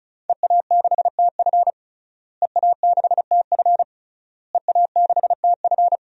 Morse code, heard as a keyed tone, about 36 wpm, 700 Hz. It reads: EA6TF EA6TF EA6TF
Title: DXCC-100 Call Signs - 5 Characters - Repeated 3x - 35wpm